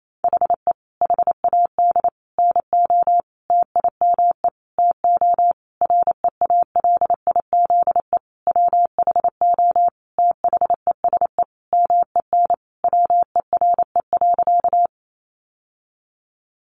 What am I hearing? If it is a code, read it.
Morse code, heard as a keyed tone, about 28 wpm, 700 Hz. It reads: HE HAD NO TIME TO REALIZE WHO THESE MEN WERE.
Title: Sentences from Top 1000 Words 28wpm